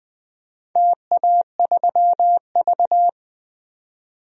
Morse code, heard as a keyed tone, about 20 wpm, 700 Hz. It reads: TA3V
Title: DXCC-100 Call Signs - 4 Characters - 20wpm